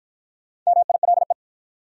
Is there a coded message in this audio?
Yes: Morse code from 0.7 to 1.3 s, heard as a keyed tone, about 53 wpm, 700 Hz. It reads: MILE